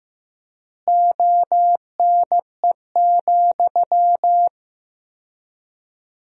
Morse code, heard as a keyed tone, about 15 wpm, 700 Hz. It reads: ONE,
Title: 4x Word-Spacing - Sentences from Top 200 Words - 15wpm